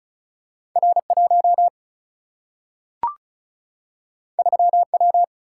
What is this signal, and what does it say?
Morse code, heard as a keyed tone, about 36 wpm, 700 Hz. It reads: R1 3W
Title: Single-Letter Single-Number 35wpm